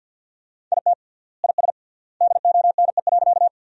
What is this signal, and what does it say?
Morse code, heard as a keyed tone, about 53 wpm, 700 Hz. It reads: IT IS DONE.
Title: 3x Word-Spacing - Sentences from Top 500 Words - 50wpm